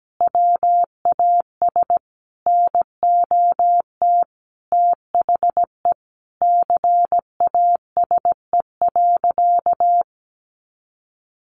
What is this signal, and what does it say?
Morse code, heard as a keyed tone, about 17 wpm, 700 Hz. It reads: WAS NOT THE CASE.
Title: Sentences from Top 1200 Words 17wpm